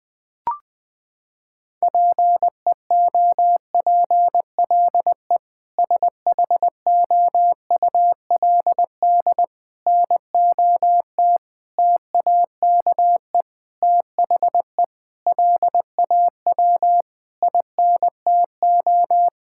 Morse code, heard as a keyed tone, about 20 wpm, 700 Hz. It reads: PEOPLE SHOULD NOT TAKE THE LAW INTO
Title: Sentences with 8-Word Phrases - 20wpm